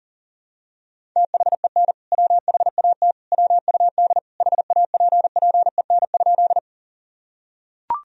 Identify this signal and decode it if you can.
Morse code, heard as a keyed tone, about 40 wpm, 700 Hz. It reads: THEN WHAT WUD HAPPEN?
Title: Sentences from Top 400 Words and Abbreviations 40wpm